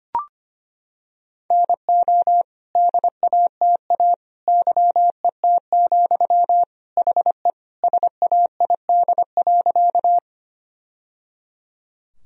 Morse code, heard as a keyed tone, about 25 wpm, 700 Hz. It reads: NO DATA YET, HE SAID.
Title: Sentences from Top 1000 Words - Review Part 3 - 25wpm